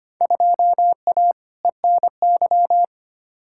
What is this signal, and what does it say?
Morse code, heard as a keyed tone, about 25 wpm, 700 Hz. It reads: 2A ENY
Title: Field Day Exchanges Encore - 25wpm